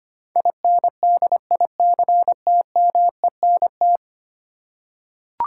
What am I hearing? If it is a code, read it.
Morse code, heard as a keyed tone, about 25 wpm, 700 Hz. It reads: INDICTMENT
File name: 10 Letter Words - First 3 Letters Given - 25wpm